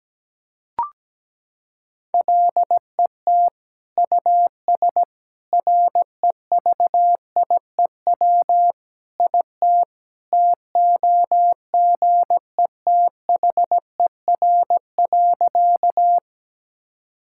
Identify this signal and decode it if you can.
Morse code, heard as a keyed tone, about 17 wpm, 700 Hz. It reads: LET US REVIEW IT TOGETHER.
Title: Sentences from Top 1000 Words - Review Part 6 - 17wpm